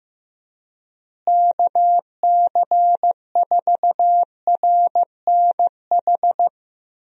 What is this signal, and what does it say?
Morse code, heard as a keyed tone, about 15 wpm, 700 Hz. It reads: KC4RNH
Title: USA Call Signs - 6 Characters - Repeated 3x - 15wpm